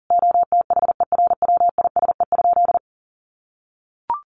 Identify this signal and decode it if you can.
Morse code, heard as a keyed tone, about 40 wpm, 700 Hz. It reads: OTHERWISE?
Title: Sentences from Top 1200 Words 40wpm